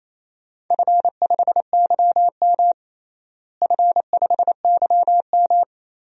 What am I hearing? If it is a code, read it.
Morse code, heard as a keyed tone, about 28 wpm, 700 Hz. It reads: F5YM F5YM
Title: DXCC-100 Call Signs - 4 Characters - Repeated 2x - 28wpm